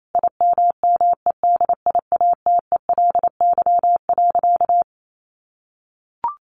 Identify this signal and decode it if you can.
Morse code, heard as a keyed tone, about 28 wpm, 700 Hz. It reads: IMMEDIATELY.
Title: Sentences from Top 1100 Words 28wpm